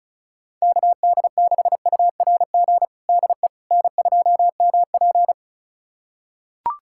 Morse code, heard as a keyed tone, about 34 wpm, 700 Hz. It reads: KD6URG DE N2MP
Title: USA Callsign DE Callsign 35wpm